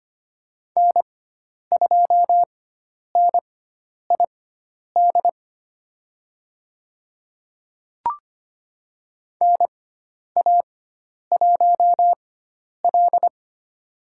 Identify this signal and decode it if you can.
Morse code, heard as a keyed tone, about 25 wpm, spaced as for 10 wpm, 700 Hz. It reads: N2NID NA1L
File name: Lesson 16g - Letter c - Call Signs 25wpm